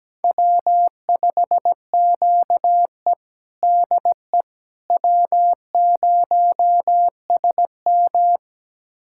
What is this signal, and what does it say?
Morse code, heard as a keyed tone, about 17 wpm, 700 Hz. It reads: W5QE DE W0SM
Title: USA Callsign DE Callsign - 4 Characters - 17wpm